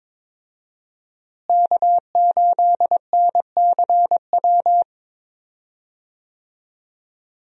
Morse code, heard as a keyed tone, about 21 wpm, 700 Hz. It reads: K8NCW
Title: USA Call Signs - 5 Characters - Repeated 2x - 22wpm